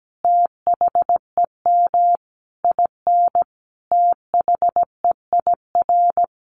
Morse code, heard as a keyed tone, about 17 wpm, 700 Hz. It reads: THEM IN THEIR
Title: Sentences from Top 1100 Words 17wpm